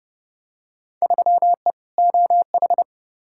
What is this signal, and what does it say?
Morse code, heard as a keyed tone, about 30 wpm, 700 Hz. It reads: 3E OH
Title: Field Day Exchanges - 30wpm